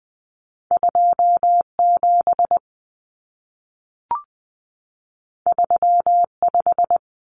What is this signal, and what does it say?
Morse code, heard as a keyed tone, about 20 wpm, 700 Hz. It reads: 27 35
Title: Common Cut Numbers - 2-Digit 20wpm